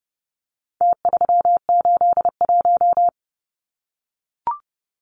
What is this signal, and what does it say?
Morse code, heard as a keyed tone, about 29 wpm, 700 Hz. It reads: T381
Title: Numbers - Contest Style - 4-Digit 30wpm